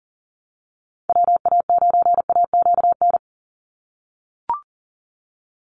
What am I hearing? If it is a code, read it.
Morse code, heard as a keyed tone, about 40 wpm, 700 Hz. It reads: WA9AQN
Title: Call Signs - 40wpm